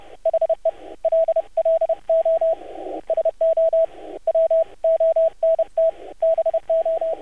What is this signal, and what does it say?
Morse code, heard as a keyed tone, about 30 wpm, spaced as for 26 wpm, 650 Hz, with radio noise. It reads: HELLO SO WONT BO
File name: archived / CRSnet